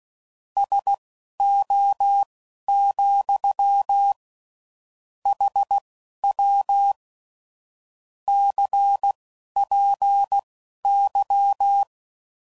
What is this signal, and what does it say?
Morse code, heard as a keyed tone, about 16 wpm, spaced as for 11 wpm, 800 Hz. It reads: SO, HW CPY
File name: CW Code Groups @15WPM / QSOs / WD0BGP Spaced